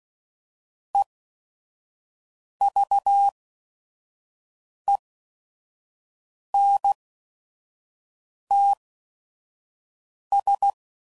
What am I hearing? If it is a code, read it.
Morse code, heard as a keyed tone, about 16 wpm, spaced as for 5 wpm, 800 Hz. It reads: EVENTS
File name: CW Code Groups @5WPM / Technical Subjects / XC